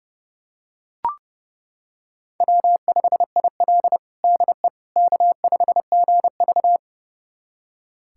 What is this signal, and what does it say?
Morse code, heard as a keyed tone, about 32 wpm, 700 Hz. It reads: W5IL DE K5GV
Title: USA Callsign DE Callsign - 4 Characters - 30wpm